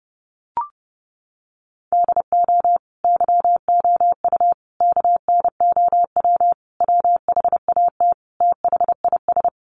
Morse code, heard as a keyed tone, about 30 wpm, 700 Hz. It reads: DO YOU KNOW WHAT THIS
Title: Sentences from Top 1500 Words 30wpm